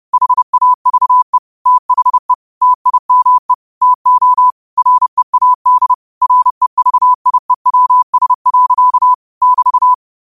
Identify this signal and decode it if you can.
Morse code, heard as a keyed tone, about 30 wpm, 1000 Hz. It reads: HAVE THE TIME TO READ REVIEWS. <BT>